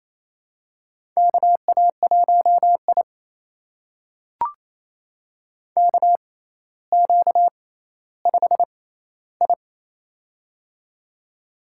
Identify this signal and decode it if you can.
Morse code, heard as a keyed tone, about 28 wpm, spaced as for 14 wpm, 700 Hz. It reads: KA1I KQ5I
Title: USA Call Signs - 4 Characters - 6x Character Spacing - 28wpm